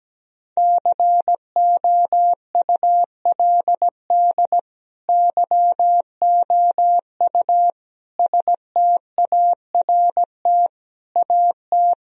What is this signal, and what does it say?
Morse code, heard as a keyed tone, about 17 wpm, 700 Hz. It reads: COULD YOU START AT
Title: Sentences from Top 1000 Words - Review Part 1 - 17wpm